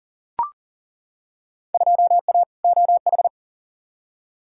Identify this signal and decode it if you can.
Morse code, heard as a keyed tone, about 37 wpm, 700 Hz. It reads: 2A OH
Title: Field Day Exchanges - 40wpm